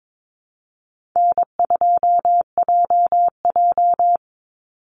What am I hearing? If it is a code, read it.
Morse code, heard as a keyed tone, about 22 wpm, 700 Hz. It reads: N2JJ
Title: Easy Call Signs Repeated 2x - 22wpm